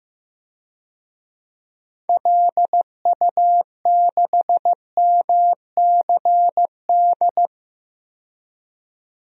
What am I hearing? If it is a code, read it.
Morse code, heard as a keyed tone, about 16 wpm, 700 Hz. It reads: LU6MCD
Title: DXCC-100 Call Signs - 6 Characters - 15wpm